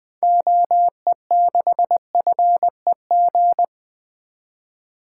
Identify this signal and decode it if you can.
Morse code, heard as a keyed tone, about 21 wpm, 700 Hz. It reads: OE6FEG
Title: 2023 SOTA Activator Call Signs - 20wpm